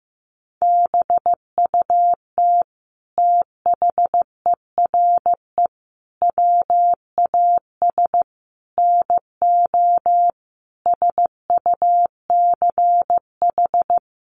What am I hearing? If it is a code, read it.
Morse code, heard as a keyed tone, about 15 wpm, 700 Hz. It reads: BUT THERE WAS NO SUCH
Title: Sentences from Top 1500 Words 15wpm